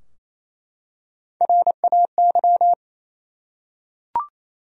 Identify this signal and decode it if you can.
Morse code, heard as a keyed tone, about 28 wpm, 700 Hz. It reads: RAY
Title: USA Common Names - Max 3 Letters - 28wpm